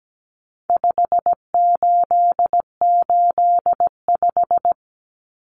Morse code, heard as a keyed tone, about 17 wpm, 700 Hz. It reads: 5885
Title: Numbers - 4-Digit 17wpm